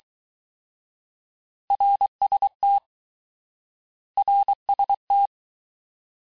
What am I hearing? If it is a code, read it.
Morse code, heard as a keyed tone, about 23 wpm, 800 Hz. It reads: RST RST
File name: cours-du-lundi files / cours-1852-abreviations-vitesse-17MPM-puis-22MPM